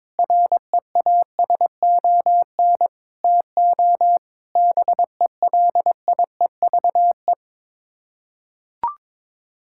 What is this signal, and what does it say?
Morse code, heard as a keyed tone, about 22 wpm, 700 Hz. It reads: REASON TO BELIEVE
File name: practice / 22wpm / Sets of 6 Words - 22wpm